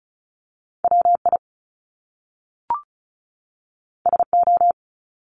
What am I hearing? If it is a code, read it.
Morse code, heard as a keyed tone, about 36 wpm, 700 Hz. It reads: WI SO
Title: Top Two Letter Combinations in Words 35wpm